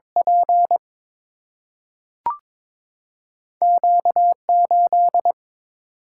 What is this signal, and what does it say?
Morse code, heard as a keyed tone, about 22 wpm, 700 Hz. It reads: P Q 8
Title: Single-Letter Single-Number 22wpm